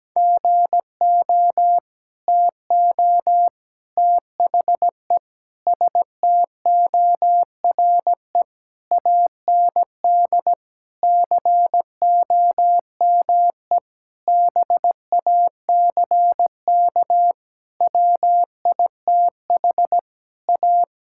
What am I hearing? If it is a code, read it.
Morse code, heard as a keyed tone, about 17 wpm, 700 Hz. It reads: GO TO THE STORE AND COME BACK WITH A
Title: Sentences from Top 1000 Words - Review Part 5 - 17wpm